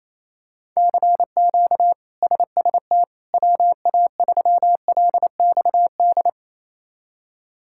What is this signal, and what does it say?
Morse code, heard as a keyed tone, about 28 wpm, 700 Hz. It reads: CQ SST WA3LXD
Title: K1USN SST QSO Practice - 28wpm